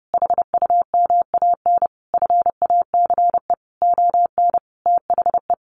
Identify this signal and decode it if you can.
Morse code, heard as a keyed tone, about 28 wpm, 700 Hz. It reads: HUMAN FACE ON THE
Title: Sentences from Top 1200 Words 30wpm